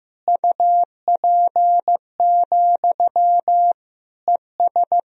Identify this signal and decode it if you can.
Morse code, heard as a keyed tone, about 15 wpm, 700 Hz. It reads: UP, ES
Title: Sentences from Top 500 Words and Abbreviations 15wpm